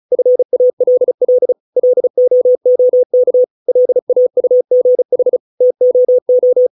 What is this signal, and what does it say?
Morse code, heard as a keyed tone, about 34 wpm, 500 Hz. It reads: FALL LOOK LAUGH TOO
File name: 632 words / 632 Words 35WPM